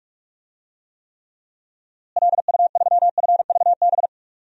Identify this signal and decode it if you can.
Morse code, heard as a keyed tone, about 43 wpm, 700 Hz. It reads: LU3FVB
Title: DXCC-100 Call Signs - 6 Characters - Repeated 3x - 45wpm